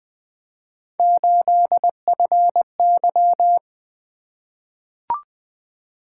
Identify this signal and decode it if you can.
Morse code, heard as a keyed tone, about 20 wpm, 700 Hz. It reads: 8FY
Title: Three Letter-Number Combinations - 20wpm